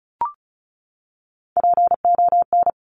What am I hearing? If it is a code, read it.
Morse code, heard as a keyed tone, about 36 wpm, 700 Hz. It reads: PON
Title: Top Three Letter Combinations in Words 35wpm